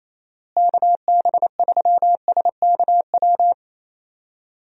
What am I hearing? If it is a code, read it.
Morse code, heard as a keyed tone, about 28 wpm, 700 Hz. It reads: KB3SKW
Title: USA Call Signs - 6 Characters - Repeated 3x - 28wpm